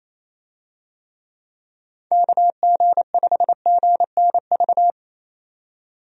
Morse code, heard as a keyed tone, about 28 wpm, 700 Hz. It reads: KG5GNV
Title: USA Call Signs - 6 Characters - 28wpm